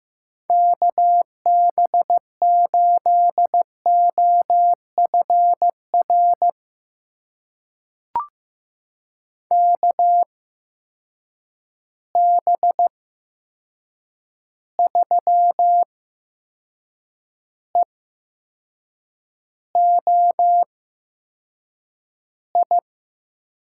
Morse code, heard as a keyed tone, about 16 wpm, spaced as for 6 wpm, 700 Hz. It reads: KB8OFR KB3EOI
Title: USA Call Signs - 6 Characters - 8x Character Spacing - 15wpm